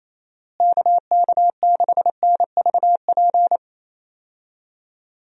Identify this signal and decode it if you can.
Morse code, heard as a keyed tone, about 28 wpm, 700 Hz. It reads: KK6NVP
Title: USA Call Signs - 6 Characters - Repeated 3x - 28wpm